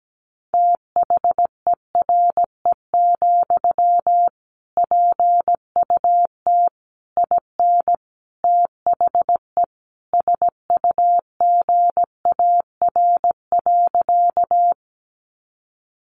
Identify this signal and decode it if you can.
Morse code, heard as a keyed tone, about 17 wpm, 700 Hz. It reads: THERE, PUT IN THE SUGAR.
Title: Sentences from Top 1200 Words 17wpm